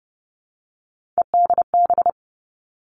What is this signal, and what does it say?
Morse code, heard as a keyed tone, about 30 wpm, 700 Hz. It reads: EDB
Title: Three Challenging Letter-Number Combinations 30wpm